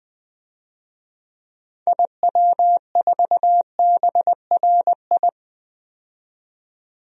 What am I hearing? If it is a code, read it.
Morse code, heard as a keyed tone, about 20 wpm, 700 Hz. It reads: IW4BRI
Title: DXCC-100 Call Signs - 6 Characters - 20wpm